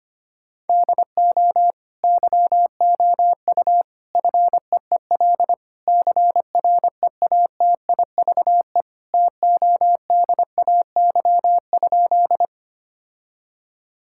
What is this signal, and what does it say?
Morse code, heard as a keyed tone, about 25 wpm, 700 Hz. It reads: DO YOU FEEL CREATIVE TODAY?
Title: Sentences from Top 1000 Words - Review Part 5 - 25wpm